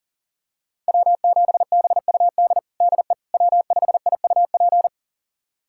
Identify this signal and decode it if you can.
Morse code, heard as a keyed tone, about 37 wpm, 700 Hz. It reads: W7BUD DE W5IUP
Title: USA Callsign DE Callsign - 5 Characters - 40wpm